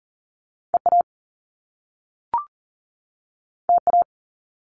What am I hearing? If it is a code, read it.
Morse code, heard as a keyed tone, about 41 wpm, 700 Hz. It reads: EA TA
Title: Top Two Letter Combinations in Words 40wpm